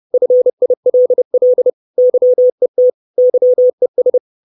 Morse code, heard as a keyed tone, about 30 wpm, 500 Hz. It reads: FILL YET YES
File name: 632 Words 30WPM